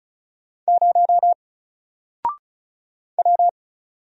Morse code, heard as a keyed tone, about 36 wpm, 700 Hz. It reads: 0 W
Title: Single Letter-Number Alt 35wpm